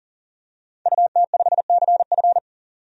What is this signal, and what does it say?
Morse code, heard as a keyed tone, about 40 wpm, 700 Hz. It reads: UT5CF
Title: DXCC-100 Call Signs - 5 Characters - 40wpm